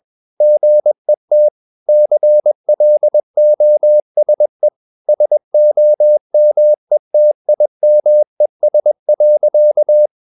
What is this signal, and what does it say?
Morse code, heard as a keyed tone, about 21 wpm, 600 Hz. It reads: GET CLOSE SOMETIMES.
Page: CW med Gnister